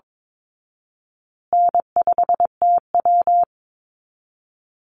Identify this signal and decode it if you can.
Morse code, heard as a keyed tone, about 22 wpm, 700 Hz. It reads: N5TW
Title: Easy Call Signs Repeated 2x - 22wpm